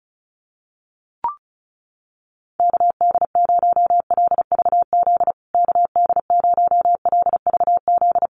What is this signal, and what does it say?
Morse code, heard as a keyed tone, about 34 wpm, 700 Hz. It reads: KD0LVZ KD0LVZ
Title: USA Call Signs Repeated 3x 35wpm